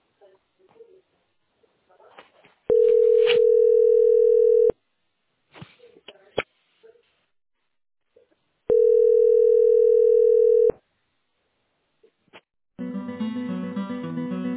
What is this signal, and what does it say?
Ringback tone (North America), 440 + 480 Hz.